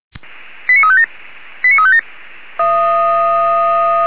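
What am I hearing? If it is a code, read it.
Five-tone selective call (ZVEI), heard as quick runs of five tones, tones of about 70 ms. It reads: 89357 89357
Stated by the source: ZVEI five-tone call 89485 received via SDR by the Feuerwehr Orlamünde fire brigade.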